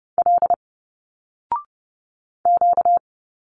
Morse code, heard as a keyed tone, about 30 wpm, 700 Hz. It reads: L Q
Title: Single Letters 30wpm